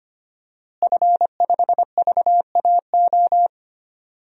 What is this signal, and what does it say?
Morse code, heard as a keyed tone, about 25 wpm, 700 Hz. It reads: F5VAO
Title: DXCC-100 Call Signs - 5 Characters - Repeated 3x - 25wpm